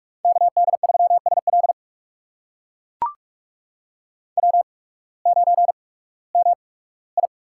Morse code, heard as a keyed tone, about 44 wpm, spaced as for 21 wpm, 690 Hz. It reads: KD3SL W9MI
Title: USA Call Signs - 5 Characters - 8x Character Spacing - 45wpm